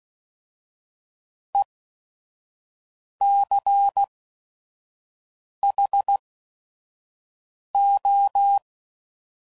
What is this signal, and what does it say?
Morse code, heard as a keyed tone, about 16 wpm, spaced as for 5 wpm, 800 Hz. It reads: ECHO
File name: Prefixes / By First Letter / Es